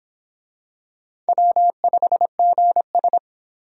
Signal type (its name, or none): Morse code